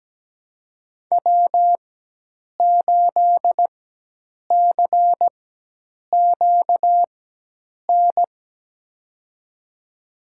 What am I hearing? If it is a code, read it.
Morse code, heard as a keyed tone, about 17 wpm, spaced as for 8 wpm, 700 Hz. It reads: W8CQN
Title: USA Call Signs - 5 Characters - 4x Character Spacing - 17wpm